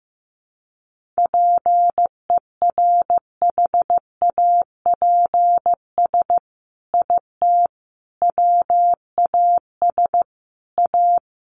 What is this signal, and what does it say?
Morse code, heard as a keyed tone, about 15 wpm, 700 Hz. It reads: PERHAPS IT WAS A
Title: Sentences from Top 1400 Words 15wpm